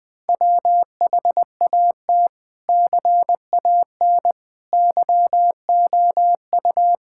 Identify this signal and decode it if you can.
Morse code, heard as a keyed tone, about 20 wpm, 700 Hz. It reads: WHAT CAN YOU
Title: Sentences from Top 1000 Words - Review Part 6 - 20wpm